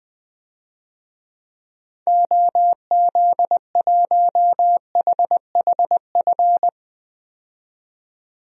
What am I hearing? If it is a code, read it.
Morse code, heard as a keyed tone, about 20 wpm, 700 Hz. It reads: OZ1HHF
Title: DXCC-100 Call Signs - 6 Characters - 20wpm